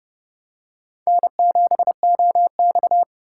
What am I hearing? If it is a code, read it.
Morse code, heard as a keyed tone, about 29 wpm, 700 Hz. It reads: N7OX
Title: USA Call Signs - 4 Characters - 4x Character Spacing - 30wpm